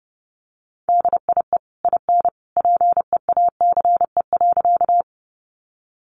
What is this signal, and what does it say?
Morse code, heard as a keyed tone, about 30 wpm, 700 Hz. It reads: DIE IN PEACE.
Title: Sentences from Top 1500 Words 30wpm